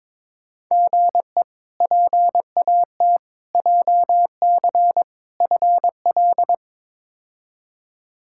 Morse code, heard as a keyed tone, about 21 wpm, 700 Hz. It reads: GE PAT JC FL
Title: K1USN SST QSO Practice - 22wpm